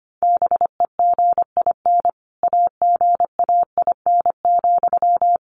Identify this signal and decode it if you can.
Morse code, heard as a keyed tone, about 25 wpm, 700 Hz. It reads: BEGIN AGAIN,
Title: Sentences from Top 800 Words 25wpm